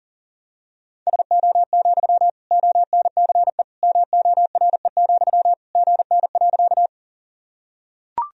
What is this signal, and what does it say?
Morse code, heard as a keyed tone, about 40 wpm, 700 Hz. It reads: SO, ONCE MORE, GN.
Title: Sentences from Top 300 Words and Abbreviations 40wpm